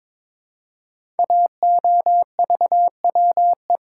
Morse code, heard as a keyed tone, about 22 wpm, 700 Hz. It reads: AOVWE
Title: Code Groups - 5 Letters - 22wpm